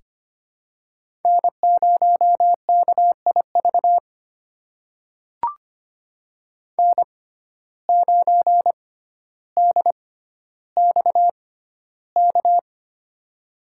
Morse code, heard as a keyed tone, about 25 wpm, spaced as for 13 wpm, 700 Hz. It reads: N0KIV N9DXK
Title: USA Call Signs - 5 Characters - 6x Character Spacing - 25wpm